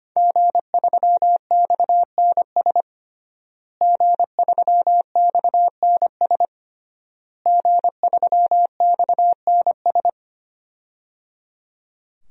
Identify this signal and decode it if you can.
Morse code, heard as a keyed tone, about 25 wpm, 700 Hz. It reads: G3XNS G3XNS G3XNS
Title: DXCC-100 Call Signs - 5 Characters - Repeated 3x - 25wpm